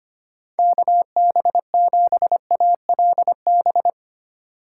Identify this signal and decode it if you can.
Morse code, heard as a keyed tone, about 25 wpm, 700 Hz. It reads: KB7ALB
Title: USA Call Signs - 6 Characters - 4x Character Spacing - 25wpm